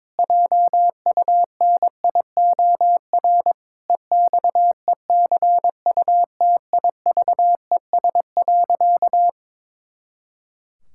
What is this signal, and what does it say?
Morse code, heard as a keyed tone, about 22 wpm, 700 Hz. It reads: JUNIOR EXECUTIVES.
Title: Pangram Challenge v2 - 22wpm